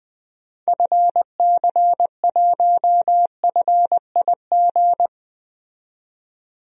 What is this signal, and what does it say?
Morse code, heard as a keyed tone, about 20 wpm, 700 Hz. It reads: FC1FIG
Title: DXCC-100 Call Signs - 6 Characters - 20wpm